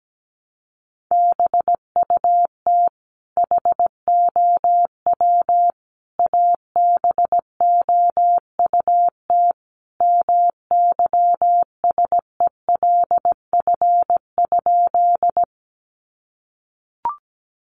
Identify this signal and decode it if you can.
Morse code, heard as a keyed tone, about 17 wpm, 700 Hz. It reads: BUT HOW ABOUT MYSELF?
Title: Sentences from Top 1300 Words 17wpm